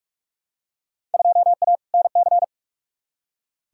Morse code, heard as a keyed tone, about 41 wpm, 700 Hz. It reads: 2A NC
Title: Field Day Exchanges - 45wpm